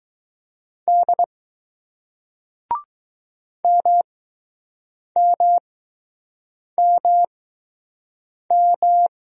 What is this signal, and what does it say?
Morse code, heard as a keyed tone, about 20 wpm, spaced as for 6 wpm, 700 Hz. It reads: D MMMM